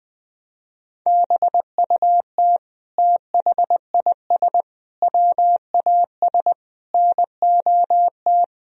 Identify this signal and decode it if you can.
Morse code, heard as a keyed tone, about 20 wpm, 700 Hz. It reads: BUT THIS WAS NOT
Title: Sentences from Top 1000 Words - Review Part 3 - 20wpm